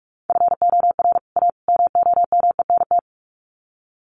Morse code, heard as a keyed tone, about 44 wpm, 700 Hz. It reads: FOR A MOMENT
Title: Sets of 3 Words 45wpm